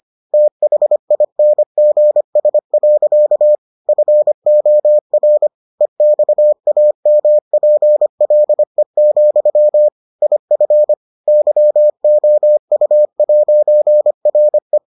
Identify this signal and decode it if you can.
Morse code, heard as a keyed tone, about 25 wpm, 600 Hz. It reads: THINGS. FOR EXAMPLE, IF YOU'RE